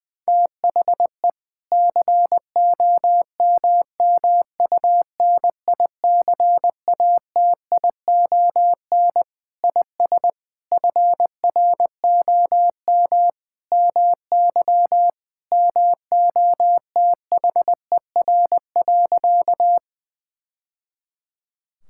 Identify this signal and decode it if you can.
Morse code, heard as a keyed tone, about 20 wpm, 700 Hz. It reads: THE COMMUNICATION IS FROM MY MOTHER.
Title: Sentences from Top 1000 Words - Review Part 6 - 20wpm